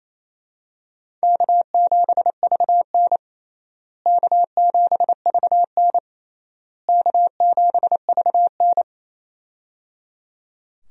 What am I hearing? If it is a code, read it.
Morse code, heard as a keyed tone, about 28 wpm, 700 Hz. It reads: K7VN K7VN K7VN
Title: USA Call Signs - 4 Characters - Repeated 3x - 28wpm